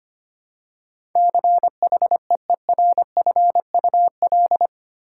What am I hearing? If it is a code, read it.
Morse code, heard as a keyed tone, about 25 wpm, 700 Hz. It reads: CHEERFUL
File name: 8 Letter Words - First 3 Letters Given - 25wpm